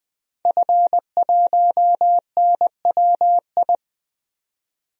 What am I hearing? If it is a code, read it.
Morse code, heard as a keyed tone, about 20 wpm, 700 Hz. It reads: F1NWI
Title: DXCC-100 Call Signs - 5 Characters - 20wpm